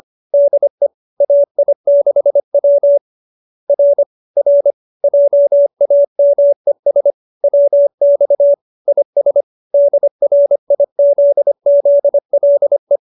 Morse code, heard as a keyed tone, about 25 wpm, 570 Hz. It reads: DE AI6W R R JAMES WX IS DRIZZLE
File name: qso207b 25